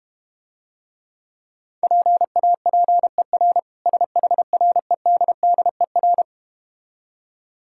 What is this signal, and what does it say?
Morse code, heard as a keyed tone, about 32 wpm, 700 Hz. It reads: PAPER SHREDDER